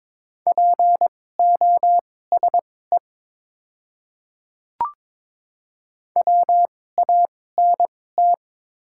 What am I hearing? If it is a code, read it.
Morse code, heard as a keyed tone, about 22 wpm, spaced as for 16 wpm, 690 Hz. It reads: POSE WANT
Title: 2x Character Spacing - Top 500 Words - 22wpm